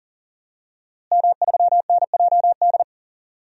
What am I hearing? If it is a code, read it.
Morse code, heard as a keyed tone, about 40 wpm, 700 Hz. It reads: M3NJD
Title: DXCC-100 Call Signs - 5 Characters - Repeated 2x - 40wpm